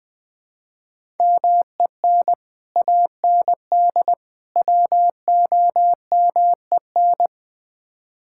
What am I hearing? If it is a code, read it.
Morse code, heard as a keyed tone, about 20 wpm, 700 Hz. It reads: MEN AND WOMEN
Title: Build A Phrase - 3-Word Phrases - 20wpm